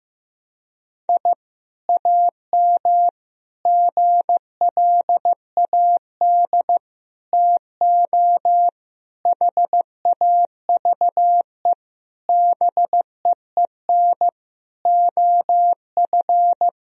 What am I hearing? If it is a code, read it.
Morse code, heard as a keyed tone, about 15 wpm, 700 Hz. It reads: I AM GLAD TO HAVE BEEN OF
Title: Sentences from Top 1000 Words - Review Part 4 - 15wpm